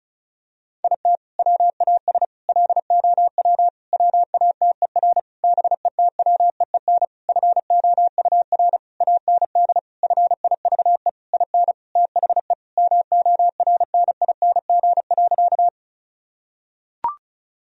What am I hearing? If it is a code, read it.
Morse code, heard as a keyed tone, about 34 wpm, 700 Hz. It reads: IT WAS LOW WATER BETWEEN FOUR AND FIVE IN THE MORNING.
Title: Sentences from Top 1000 Words - Review Part 2 - 35wpm